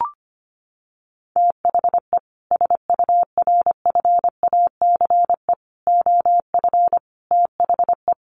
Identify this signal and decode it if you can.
Morse code, heard as a keyed tone, about 25 wpm, 700 Hz. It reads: THE SURFACE OF THE
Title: Sets of 4 Words 25wpm